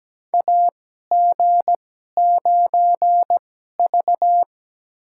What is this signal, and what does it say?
Morse code, heard as a keyed tone, about 17 wpm, spaced as for 12 wpm, 700 Hz. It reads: AG9V
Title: USA Call Signs - 4 Characters - 2x Character Spacing - 17wpm